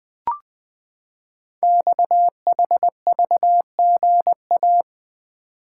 Morse code, heard as a keyed tone, about 20 wpm, 700 Hz. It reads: XHVGA
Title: Code Groups - 5 Letters - 20wpm